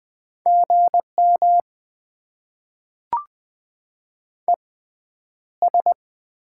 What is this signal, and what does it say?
Morse code, heard as a keyed tone, about 20 wpm, spaced as for 10 wpm, 690 Hz. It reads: GM ES
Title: Top 20 QSO Elements - 6x Character Spacing - 20wpm